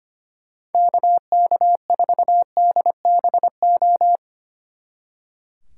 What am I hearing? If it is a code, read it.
Morse code, heard as a keyed tone, about 31 wpm, 700 Hz. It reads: KK4DBO
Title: Speed Racing - USA 2x3 Call Signs 25wpm